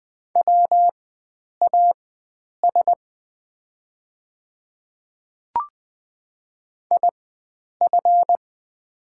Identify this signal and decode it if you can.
Morse code, heard as a keyed tone, about 20 wpm, spaced as for 9 wpm, 700 Hz. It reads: WAS IF
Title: 4x Character Spacing - Top 100 Words - 20wpm